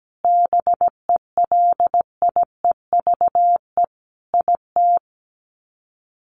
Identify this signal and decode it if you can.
Morse code, heard as a keyed tone, about 17 wpm, 700 Hz. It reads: BELIEVE IT
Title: Sets of 5 Words Encore 17wpm